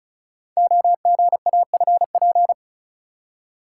Morse code, heard as a keyed tone, about 36 wpm, 700 Hz. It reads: OGAFP
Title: Code Groups - 5 Letters - 35wpm